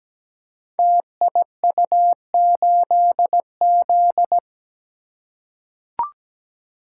Morse code, heard as a keyed tone, about 17 wpm, 700 Hz. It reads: TIU8Z
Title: Code Groups - 5 Letter:Numbers - 17wpm